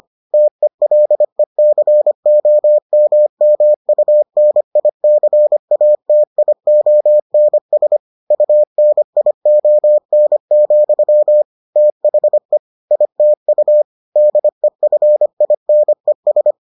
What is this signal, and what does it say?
Morse code, heard as a keyed tone, about 25 wpm, 600 Hz. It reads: TELECOMMUNICATIONS UNION, THE ITU DEFINES